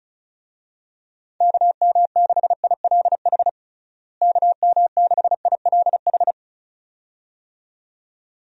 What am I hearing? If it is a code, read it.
Morse code, heard as a keyed tone, about 34 wpm, 700 Hz. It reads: KM6ILH KM6ILH
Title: USA Call Signs - 6 Characters - Repeated 2x - 35wpm